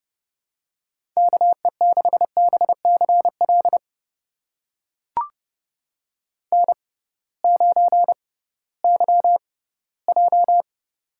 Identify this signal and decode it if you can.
Morse code, heard as a keyed tone, about 30 wpm, spaced as for 18 wpm, 700 Hz. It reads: KE6BCL N9YJ
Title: USA Call Signs - 6x Character Spacing - 30wpm